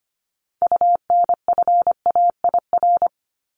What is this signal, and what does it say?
Morse code, heard as a keyed tone, about 25 wpm, 700 Hz. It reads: UNFAIR
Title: Top 1751-2000 Words 25wpm